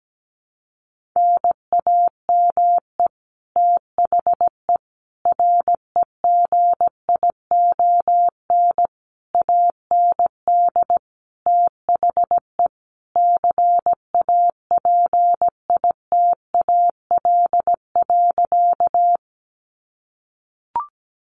Morse code, heard as a keyed tone, about 17 wpm, 700 Hz. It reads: NAME THE REGION AND THE CAPITAL.
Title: Sentences from Top 1500 Words 17wpm